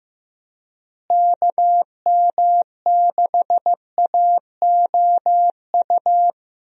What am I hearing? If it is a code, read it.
Morse code, heard as a keyed tone, about 15 wpm, 700 Hz. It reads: KM6AOU
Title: USA Call Signs - 6 Characters - Repeated 3x - 15wpm